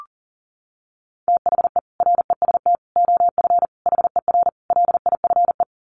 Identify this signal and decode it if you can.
Morse code, heard as a keyed tone, about 40 wpm, 700 Hz. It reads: THE REST OF HER LIFE
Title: Sets of 5 Words Encore 40wpm